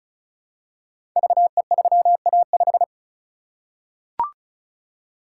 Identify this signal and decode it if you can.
Morse code, heard as a keyed tone, about 36 wpm, 690 Hz. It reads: VE3A5
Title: Code Groups - 5 Letter:Numbers - 35wpm